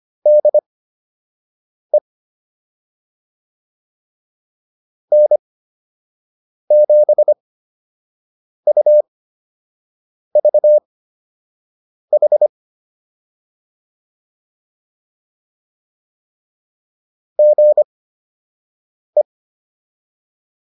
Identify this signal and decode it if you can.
Morse code, heard as a keyed tone, about 25 wpm, spaced as for 6 wpm, 600 Hz. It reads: DE N7UVH GE
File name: QSO 105 06